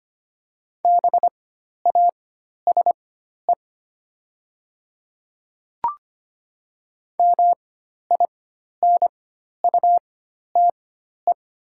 Morse code, heard as a keyed tone, about 25 wpm, spaced as for 12 wpm, 700 Hz. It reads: BASE MINUTE
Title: 4x Character Spacing - Top 500 Words - 25wpm